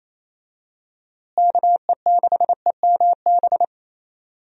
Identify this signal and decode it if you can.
Morse code, heard as a keyed tone, about 28 wpm, 700 Hz. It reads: KE6EMB
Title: USA Call Signs - 6 Characters - Repeated 2x - 28wpm